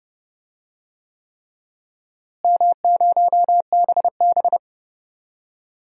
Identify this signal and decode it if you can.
Morse code, heard as a keyed tone, about 30 wpm, 700 Hz. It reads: M0BB
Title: DXCC-100 Call Signs - 4 Characters - Repeated 3x - 30wpm